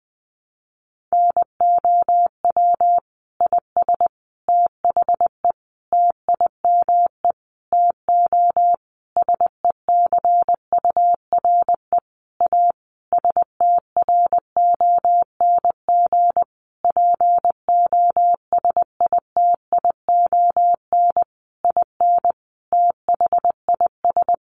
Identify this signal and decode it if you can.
Morse code, heard as a keyed tone, about 20 wpm, 700 Hz. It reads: NOW IS THE TIME TO SECURE A STRONG POSITION IN THIS
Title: Sentences from Top 1200 Words 20wpm